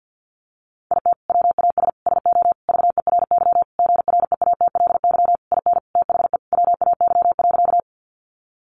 Morse code, heard as a keyed tone, about 50 wpm, 700 Hz. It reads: IT WAS SO VERY GREATLY IN THE WAY.
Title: Sentences from Top 1400 Words 50wpm